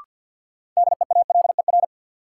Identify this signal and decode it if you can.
Morse code, heard as a keyed tone, about 52 wpm, 700 Hz. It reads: DEALER
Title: Sentences from Top 1100 Words - New Word Drill - 50wpm